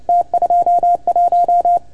Morse code, heard as a keyed tone, about 30 wpm, 690 Hz. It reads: T21
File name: Archiv / CUC310 / OK1FGD CUC 310 pileup